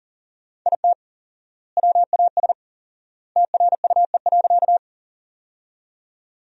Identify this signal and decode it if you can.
Morse code, heard as a keyed tone, about 40 wpm, 700 Hz. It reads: IT WAS TRUE.
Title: 4x Word-Spacing - Sentences from Top 400 Words - 40wpm